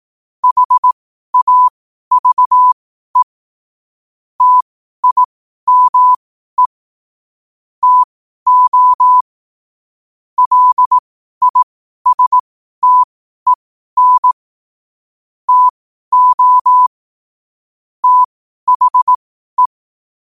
Morse code, heard as a keyed tone, about 18 wpm, spaced as for 12 wpm, 1000 Hz. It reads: HAVE TIME TO LISTEN TO THE
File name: qotdfiles / 12wpm